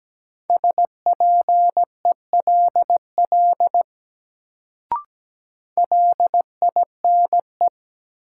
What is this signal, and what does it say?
Morse code, heard as a keyed tone, about 17 wpm, 700 Hz. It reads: SPELL LINE
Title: Sentences from Top 1000 Words - Review Part 1 - New Word Drill - 17wpm